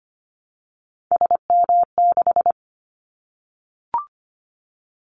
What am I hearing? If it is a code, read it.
Morse code, heard as a keyed tone, about 25 wpm, 700 Hz. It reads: SM6
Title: Callsign Prefixes for Top 25 DXCC - 25wpm